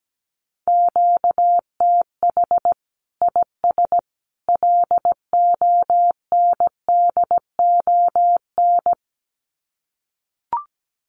Morse code, heard as a keyed tone, about 17 wpm, 700 Hz. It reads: QTH IS LONDON
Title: Common QSO Phrases 17wpm